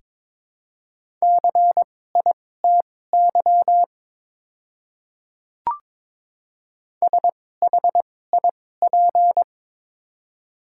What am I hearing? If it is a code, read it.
Morse code, heard as a keyed tone, about 23 wpm, spaced as for 16 wpm, 690 Hz. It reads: CITY SHIP
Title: 2x Character Spacing - Top 500 Words - 22wpm